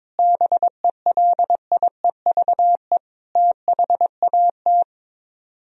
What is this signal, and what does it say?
Morse code, heard as a keyed tone, about 22 wpm, 700 Hz. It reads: BELIEVE THAT
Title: Speed-Racing - Sets of 5 Words 22wpm